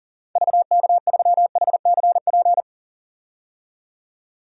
Morse code, heard as a keyed tone, about 39 wpm, 700 Hz. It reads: VK3HCP
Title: DXCC-100 Call Signs - 6 Characters - 40wpm